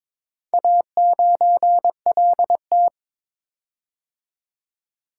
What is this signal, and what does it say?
Morse code, heard as a keyed tone, about 21 wpm, 700 Hz. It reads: A9LT
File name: practice / 22wpm / Four Letter-Number Combinations - 22wpm